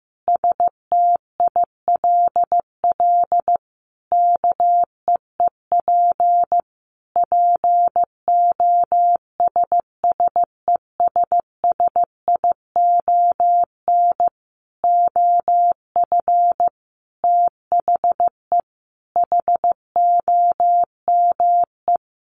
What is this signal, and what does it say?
Morse code, heard as a keyed tone, about 15 wpm, 700 Hz. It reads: STILL KEEP POSSESSION OF THE HOME
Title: Sentences from Top 1400 Words 15wpm